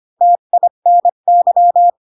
Morse code, heard as a keyed tone, about 25 wpm, 700 Hz. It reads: TINY
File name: Pinocchio / Pinocchio ch1 20 25 ik1hge